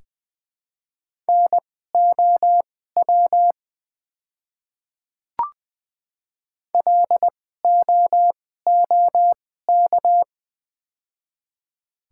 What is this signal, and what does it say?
Morse code, heard as a keyed tone, about 20 wpm, spaced as for 14 wpm, 700 Hz. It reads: NOW LOOK
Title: 2x Character Spacing - Top 100 Words - 20wpm